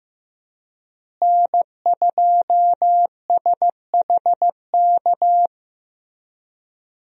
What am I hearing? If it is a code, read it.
Morse code, heard as a keyed tone, about 15 wpm, 700 Hz. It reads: N2SHK
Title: USA Call Signs - 5 Characters - Repeated 3x - 15wpm